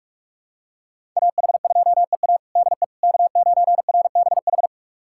Morse code, heard as a keyed tone, about 42 wpm, 700 Hz. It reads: AH2EA DE K9RBH
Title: USA Callsign DE Callsign - 5 Characters - 45wpm